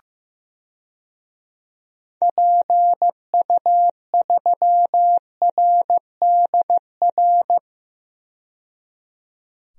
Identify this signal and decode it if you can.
Morse code, heard as a keyed tone, about 15 wpm, 700 Hz. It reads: PU3RDR
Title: DXCC-100 Call Signs - 6 Characters - 15wpm